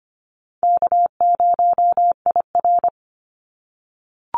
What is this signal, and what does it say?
Morse code, heard as a keyed tone, about 24 wpm, 700 Hz. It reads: K0IR
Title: Easy Call Signs - 25wpm